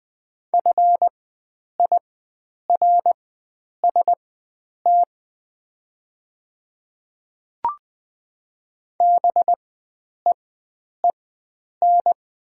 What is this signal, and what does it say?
Morse code, heard as a keyed tone, about 20 wpm, spaced as for 9 wpm, 690 Hz. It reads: FIRST BEEN
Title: 4x Character Spacing - Top 100 Words - 20wpm